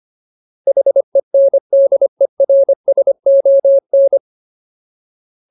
Morse code, heard as a keyed tone, about 25 wpm, 550 Hz. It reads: HENDERSON
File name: Ultimate Word List / 25WPM Ultimate